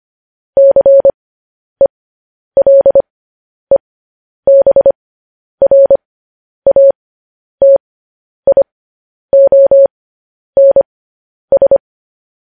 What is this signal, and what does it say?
Morse code, heard as a keyed tone, about 25 wpm, spaced as for 10 wpm, 550 Hz. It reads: CELEBRATIONS
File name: podcast / News headlines-1x-25x10